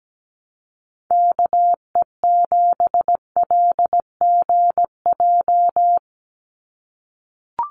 Morse code, heard as a keyed tone, about 17 wpm, 700 Hz. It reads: KE7LGJ
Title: USA Call Signs 17wpm